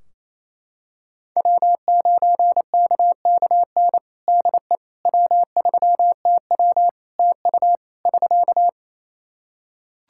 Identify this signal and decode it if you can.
Morse code, heard as a keyed tone, about 27 wpm, 700 Hz. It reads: W9KKN DE W3TW TU <SK>
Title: Practice QSO - Exchange 4 - Ending - 28wpm